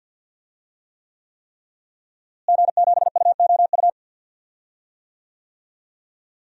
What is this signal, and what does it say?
Morse code, heard as a keyed tone, about 52 wpm, 700 Hz. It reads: G7UOU
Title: DXCC-100 Call Signs - 5 Characters - Repeated 3x - 50wpm